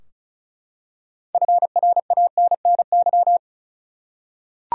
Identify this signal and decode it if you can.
Morse code, heard as a keyed tone, about 34 wpm, 700 Hz. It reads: FRANNY
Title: USA Common Names - 35wpm